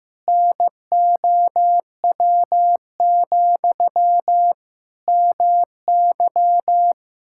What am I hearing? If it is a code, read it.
Morse code, heard as a keyed tone, about 16 wpm, 700 Hz. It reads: NOW, MY
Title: Sentences from Top 1000 Words - Review Part 5 - 15wpm